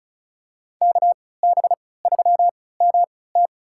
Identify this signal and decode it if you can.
Morse code, heard as a keyed tone, about 36 wpm, spaced as for 20 wpm, 700 Hz. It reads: KB3MT
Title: USA Call Signs - 5 Characters - 3x Character Spacing - 35wpm